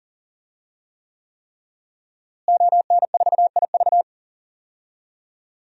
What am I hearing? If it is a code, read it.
Morse code, heard as a keyed tone, about 40 wpm, 700 Hz. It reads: ON4IV
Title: DXCC-100 Call Signs - 5 Characters - Repeated 3x - 40wpm